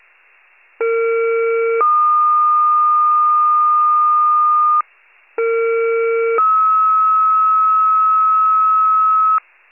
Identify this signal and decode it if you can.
Two-tone page, heard as a one-second tone, then a three-second tone, about 460 then 1220 Hz, about 460 then 1290 Hz. It is heard from 0.8 to 9.4 s.
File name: wcalert